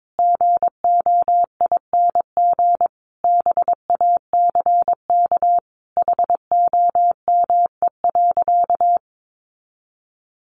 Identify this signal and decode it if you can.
Morse code, heard as a keyed tone, about 22 wpm, 700 Hz. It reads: GOING BACK HOME.
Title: Sentences from Top 500 Words 22wpm